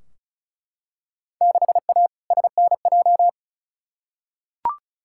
Morse code, heard as a keyed tone, about 36 wpm, 700 Hz. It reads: 6A SNJ
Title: Field Day Exchanges Encore - 35wpm